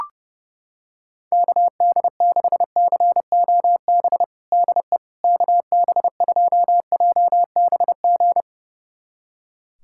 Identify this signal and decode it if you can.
Morse code, heard as a keyed tone, about 31 wpm, 700 Hz. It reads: KD6COB DE KB2JBG
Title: USA Callsign DE Callsign - 6 Characters - 30wpm